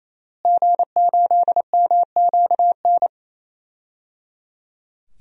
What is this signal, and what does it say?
Morse code, heard as a keyed tone, about 27 wpm, 700 Hz. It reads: G8MQN
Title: DXCC-100 Call Signs - 5 Characters - Repeated 3x - 28wpm